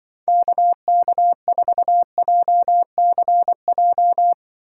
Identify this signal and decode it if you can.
Morse code, heard as a keyed tone, about 25 wpm, 700 Hz. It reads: KK4JCJ
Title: Speed Racing - USA 2x3 Call Signs 20wpm